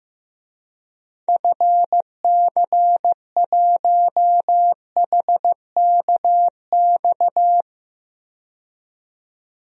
Morse code, heard as a keyed tone, about 15 wpm, 700 Hz. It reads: FC1HKX
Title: DXCC-100 Call Signs - 6 Characters - 15wpm